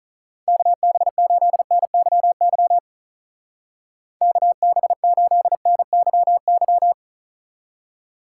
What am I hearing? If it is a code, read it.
Morse code, heard as a keyed tone, about 42 wpm, 700 Hz. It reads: KB8NYY KB8NYY
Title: Speed Racing - USA 2x3 Call Signs 35wpm